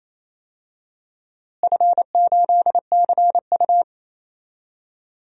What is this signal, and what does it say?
Morse code, heard as a keyed tone, about 28 wpm, 700 Hz. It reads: F8CU
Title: DXCC-100 Call Signs - 4 Characters - Repeated 2x - 28wpm